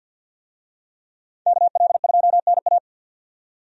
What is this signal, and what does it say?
Morse code, heard as a keyed tone, about 51 wpm, 700 Hz. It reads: KL2NA
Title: USA Call Signs - 5 Characters - 50wpm